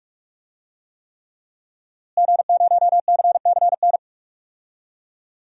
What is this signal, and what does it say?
Morse code, heard as a keyed tone, about 47 wpm, 700 Hz. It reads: G0CCN
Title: DXCC-100 Call Signs - 5 Characters - Repeated 3x - 45wpm